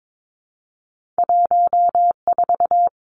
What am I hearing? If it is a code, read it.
Morse code, heard as a keyed tone, about 22 wpm, 700 Hz. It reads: 14
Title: Two Challenging Letter-Number Combinations 22wpm